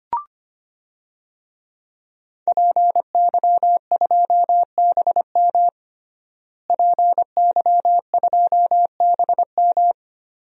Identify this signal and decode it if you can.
Morse code, heard as a keyed tone, about 25 wpm, 700 Hz. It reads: PY2BM PY2BM
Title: DXCC-100 Call Signs - 5 Characters - Repeated 3x - 25wpm